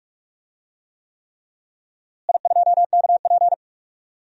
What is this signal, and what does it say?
Morse code, heard as a keyed tone, about 47 wpm, 700 Hz. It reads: I2KP
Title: DXCC-100 Call Signs - 4 Characters - Repeated 3x - 45wpm